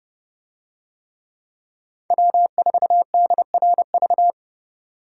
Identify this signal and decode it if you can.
Morse code, heard as a keyed tone, about 28 wpm, 700 Hz. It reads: W4DRV